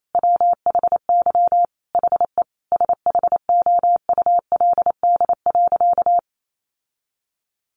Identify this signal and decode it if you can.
Morse code, heard as a keyed tone, about 28 wpm, 700 Hz. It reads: WHY HE SHOULD.
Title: Sentences from Top 700 Words 28wpm